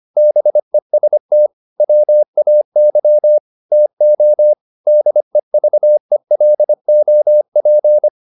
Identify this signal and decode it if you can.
Morse code, heard as a keyed tone, about 25 wpm, 600 Hz. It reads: BEST WAY TO DEVELOP